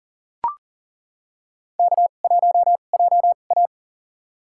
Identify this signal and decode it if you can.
Morse code, heard as a keyed tone, about 40 wpm, spaced as for 29 wpm, 700 Hz. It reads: K1JA
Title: USA Call Signs - 4 Characters - 2x Character Spacing - 40wpm